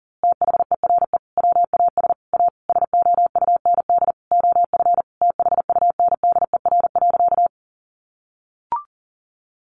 Morse code, heard as a keyed tone, about 40 wpm, 700 Hz. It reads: THERE WAS A SOUND OF THUNDER.
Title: A Sound Of Thunder 40wpm